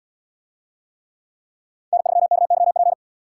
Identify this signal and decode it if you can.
Morse code, heard as a keyed tone, about 77 wpm, 700 Hz. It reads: N3KYP